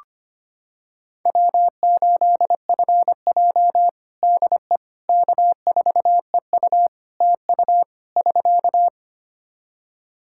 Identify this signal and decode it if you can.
Morse code, heard as a keyed tone, about 25 wpm, 700 Hz. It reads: W8FJ DE K4EU TU <SK>
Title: Practice QSO - Exchange 4 - Ending - 25wpm